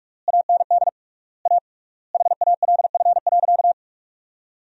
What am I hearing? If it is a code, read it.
Morse code, heard as a keyed tone, about 48 wpm, 700 Hz. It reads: AND A HALF.
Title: 3x Word-Spacing - Sentences from Top 400 Words - 45wpm